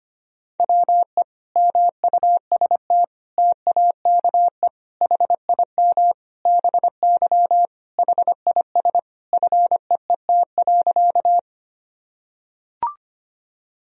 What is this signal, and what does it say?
Morse code, heard as a keyed tone, about 25 wpm, 700 Hz. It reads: WE MUST TAKE HIM BY HIS FEET.
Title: Sentences from Top 1000 Words - Review Part 1 - 25wpm